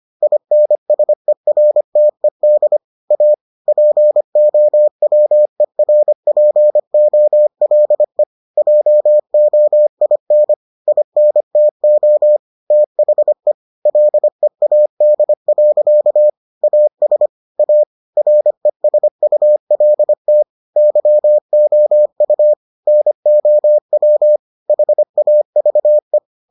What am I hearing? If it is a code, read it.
Morse code, heard as a keyed tone, about 25 wpm, 600 Hz. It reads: INSERTED A POWERPOLE JOIN INTO THE LEAD. AS A RESULT YOU NOW HAVE